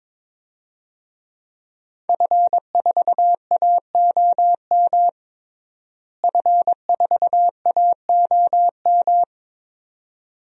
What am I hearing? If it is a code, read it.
Morse code, heard as a keyed tone, about 22 wpm, 700 Hz. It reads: F4AOM F4AOM
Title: DXCC-100 Call Signs - 5 Characters - Repeated 2x - 22wpm